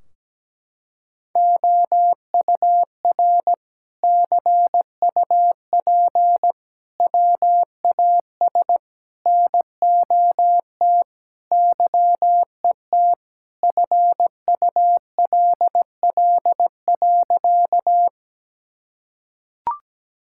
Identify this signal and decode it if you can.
Morse code, heard as a keyed tone, about 17 wpm, 700 Hz. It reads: OUR CUP WAS NOT YET FULL.
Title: Sentences from Top 1600 Words 17wpm